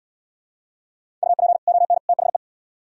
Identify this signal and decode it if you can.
Morse code, heard as a keyed tone, about 76 wpm, 700 Hz. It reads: UP ON THE